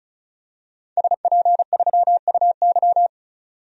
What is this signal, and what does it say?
Morse code, heard as a keyed tone, about 34 wpm, 700 Hz. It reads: SP3UY
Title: DXCC-100 Call Signs - 5 Characters - 35wpm